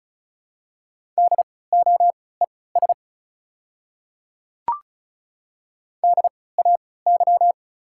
Morse code, heard as a keyed tone, about 34 wpm, spaced as for 20 wpm, 700 Hz. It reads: DOES DAY
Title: 3x Character Spacing - Top 300 Words - 35wpm